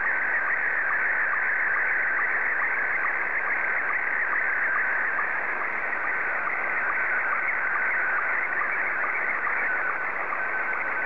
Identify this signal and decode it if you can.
SSTV (slow-scan television), Scottie 1.